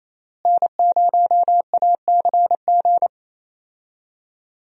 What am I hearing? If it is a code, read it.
Morse code, heard as a keyed tone, about 28 wpm, 700 Hz. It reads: N0ACG
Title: USA Call Signs - 5 Characters - Repeated 3x - 28wpm